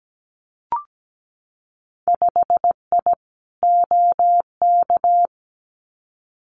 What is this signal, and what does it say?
Morse code, heard as a keyed tone, about 17 wpm, 700 Hz. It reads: 5I OK
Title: Winter Field Day Exchanges Encore 17wpm